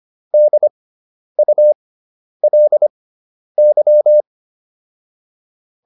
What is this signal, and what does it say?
Morse code, heard as a keyed tone, about 25 wpm, spaced as for 10 wpm, 600 Hz. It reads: DULY